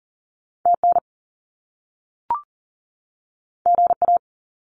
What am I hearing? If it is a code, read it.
Morse code, heard as a keyed tone, about 42 wpm, 700 Hz. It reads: TN GA
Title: US States 40wpm